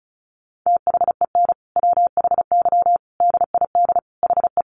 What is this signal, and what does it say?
Morse code, heard as a keyed tone, about 34 wpm, 700 Hz. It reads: THEN WHY DID HE
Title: Sentences from Top 800 Words 35wpm